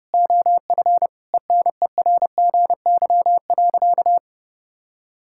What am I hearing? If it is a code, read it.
Morse code, heard as a keyed tone, about 30 wpm, 700 Hz. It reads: OF ENERGY.
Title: Sentences from Top 1000 Words - Review Part 6 - 30wpm